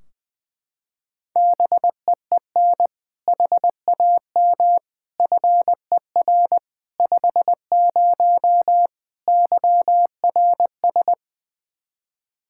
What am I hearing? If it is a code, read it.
Morse code, heard as a keyed tone, about 21 wpm, 700 Hz. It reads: BEEN HAM FER 50 YRS
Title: Practice QSO - Exchange 3 - More Info - 20wpm